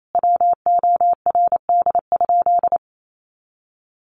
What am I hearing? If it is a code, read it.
Morse code, heard as a keyed tone, about 28 wpm, 700 Hz. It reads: WORD?
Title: Sentences from Top 1500 Words 28wpm